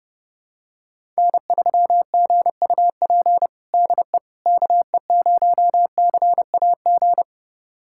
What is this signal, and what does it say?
Morse code, heard as a keyed tone, about 30 wpm, 700 Hz. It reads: N3GUP DE KE0CAG
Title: USA Callsign DE Callsign 30wpm